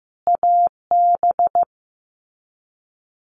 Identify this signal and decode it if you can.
Morse code, heard as a keyed tone, about 15 wpm, 700 Hz. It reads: AB
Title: Canadian Provinces and Territories - Repeated 3x - 15wpm